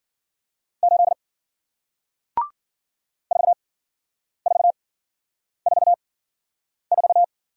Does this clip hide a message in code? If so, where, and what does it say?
Yes: Morse code from 0.8 to 7.2 s, heard as a keyed tone, about 56 wpm, spaced as for 9 wpm, 700 Hz. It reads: 8 4444